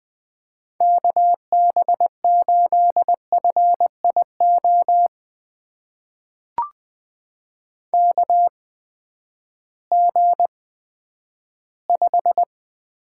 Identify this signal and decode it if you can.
Morse code, heard as a keyed tone, about 20 wpm, spaced as for 11 wpm, 700 Hz. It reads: KB8FIO KG5
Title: USA Call Signs - 6 Characters - 8x Character Spacing - 20wpm